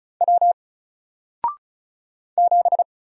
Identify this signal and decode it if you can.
Morse code, heard as a keyed tone, about 36 wpm, 700 Hz. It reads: W 7
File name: Single-Letter-Number-35wpm